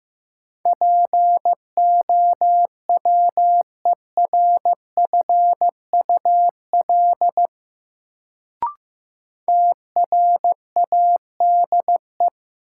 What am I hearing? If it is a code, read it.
Morse code, heard as a keyed tone, about 15 wpm, 700 Hz. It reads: POWERFUL TRADE
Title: Sets of 5 Words Encore - New Word Drill - 15wpm